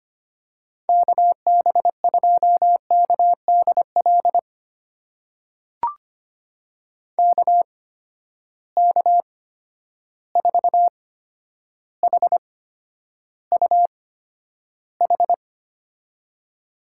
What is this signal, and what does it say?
Morse code, heard as a keyed tone, about 25 wpm, spaced as for 11 wpm, 700 Hz. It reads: KB2KDL KK4HUH
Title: USA Call Signs - 6 Characters - 8x Character Spacing - 25wpm